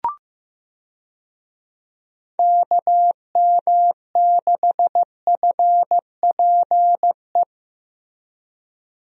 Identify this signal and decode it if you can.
Morse code, heard as a keyed tone, about 15 wpm, 700 Hz. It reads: KM6FPE
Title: USA Call Signs - 6 Characters - Repeated 2x - 15wpm